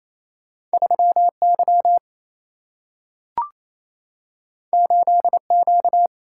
Morse code, heard as a keyed tone, about 28 wpm, 700 Hz. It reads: 3Y 8Q
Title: Single-Letter Single-Number 28wpm